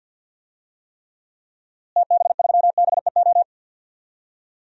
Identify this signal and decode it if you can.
Morse code, heard as a keyed tone, about 52 wpm, 700 Hz. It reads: TB3BEO
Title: DXCC-100 Call Signs - 6 Characters - Repeated 2x - 50wpm